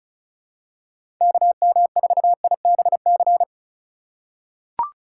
Morse code, heard as a keyed tone, about 36 wpm, 700 Hz. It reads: KM4IBC
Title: USA Call Signs - 6 Characters - 3x Character Spacing - 35wpm